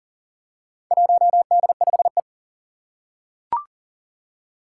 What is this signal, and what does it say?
Morse code, heard as a keyed tone, about 39 wpm, 700 Hz. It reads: 1D5E
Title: Four Challenging Letter-Number Combinations 40wpm